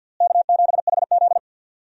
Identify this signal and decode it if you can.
Morse code, heard as a keyed tone, about 46 wpm, 700 Hz. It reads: K7HZ
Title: USA Call Signs - 4 Characters - Repeated 3x - 50wpm